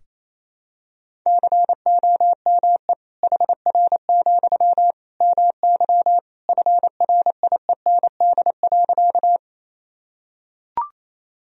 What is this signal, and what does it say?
Morse code, heard as a keyed tone, about 28 wpm, 700 Hz. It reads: COME HR, MY FRIEND.
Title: Sentences from Top 300 Words and Abbreviations 28wpm